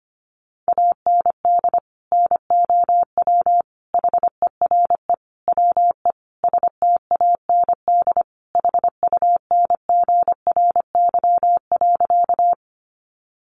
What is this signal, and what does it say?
Morse code, heard as a keyed tone, about 25 wpm, 700 Hz. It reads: AND NOW HERE WE STAND HUNGRY.
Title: Sentences from Top 1400 Words 25wpm